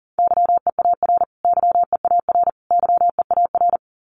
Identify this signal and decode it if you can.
Morse code, heard as a keyed tone, about 40 wpm, 700 Hz. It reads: YEAR YEAR YEAR
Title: Top 200 Words Repeated 40wpm